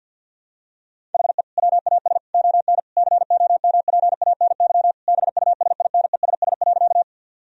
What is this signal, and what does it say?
Morse code, heard as a keyed tone, about 51 wpm, 700 Hz. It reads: HE WAS ON COMPANY BUSINESS.
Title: Sentences from Top 1000 Words - Review Part 3 - 50wpm